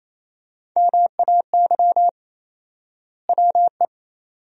Morse code, heard as a keyed tone, about 28 wpm, 700 Hz. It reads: MAY WE
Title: 4x Word-Spacing - Sentences from Top 100 Words - 28wpm